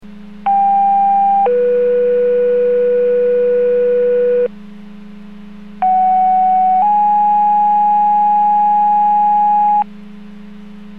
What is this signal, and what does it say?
Two-tone page, heard as a one-second tone, then a three-second tone, about 800 then 490 Hz, about 750 then 830 Hz. It is heard from 0.5 to 9.8 s.